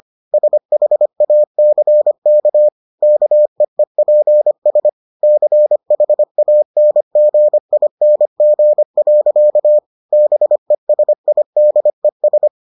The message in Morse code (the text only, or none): SHACK KEEPS CHANGING. BESIDES